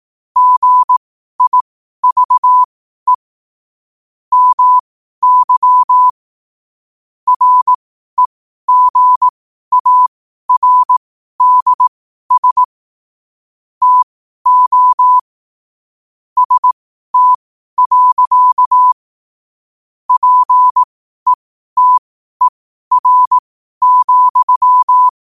Morse code, heard as a keyed tone, about 18 wpm, spaced as for 12 wpm, 1000 Hz. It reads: GIVE MY REGARDS TO ST. PETER,
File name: qotdfiles / 12wpm